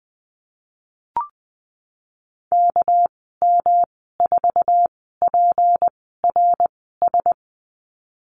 Morse code, heard as a keyed tone, about 19 wpm, spaced as for 14 wpm, 700 Hz. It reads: KM4PRS
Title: 207 Lesson 33g - Letter j - Call Signs 20wpm